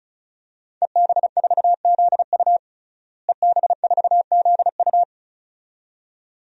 Morse code, heard as a keyed tone, about 34 wpm, 700 Hz. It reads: EB4ZU EB4ZU
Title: DXCC-100 Call Signs - 5 Characters - Repeated 2x - 35wpm